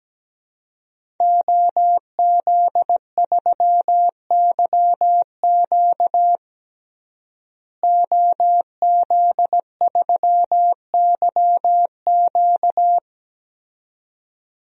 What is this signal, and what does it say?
Morse code, heard as a keyed tone, about 17 wpm, 700 Hz. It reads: OZ3YQ OZ3YQ
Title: DXCC-100 Call Signs - 5 Characters - Repeated 2x - 17wpm